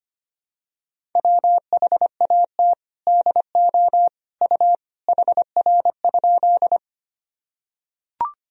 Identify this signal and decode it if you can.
Morse code, heard as a keyed tone, about 25 wpm, 700 Hz. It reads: WHAT DO U HR?
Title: Sentences from Top 300 Words and Abbreviations 25wpm